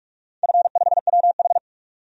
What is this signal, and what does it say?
Morse code, heard as a keyed tone, about 46 wpm, 700 Hz. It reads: F5WH
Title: DXCC-100 Call Signs - 4 Characters - 45wpm